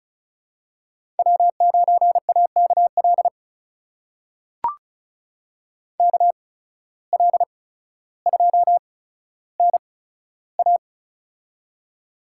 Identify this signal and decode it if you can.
Morse code, heard as a keyed tone, about 36 wpm, spaced as for 15 wpm, 700 Hz. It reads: W9AKL KL2NA
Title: USA Call Signs - 5 Characters - 8x Character Spacing - 35wpm